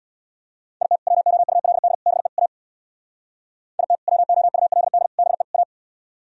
Morse code, heard as a keyed tone, about 64 wpm, 700 Hz. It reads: IT WOULD BE A IT WOULD BE A